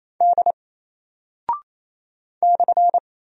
Morse code, heard as a keyed tone, about 28 wpm, 700 Hz. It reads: D /
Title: Single Character (Letter:Number:Punctuation:Prosign) 28wpm